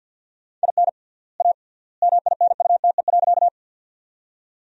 Morse code, heard as a keyed tone, about 53 wpm, 700 Hz. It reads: IN A MINUTE.
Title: 3x Word-Spacing - Sentences from Top 500 Words - 50wpm